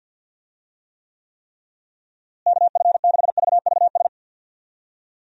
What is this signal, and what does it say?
Morse code, heard as a keyed tone, about 51 wpm, 700 Hz. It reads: KF6VVS
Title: USA Call Signs - 6 Characters - Repeated 3x - 50wpm